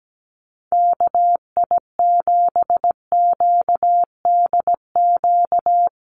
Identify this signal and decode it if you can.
Morse code, heard as a keyed tone, about 17 wpm, 700 Hz. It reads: KI7QDQ
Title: USA Call Signs Repeated 2x 17wpm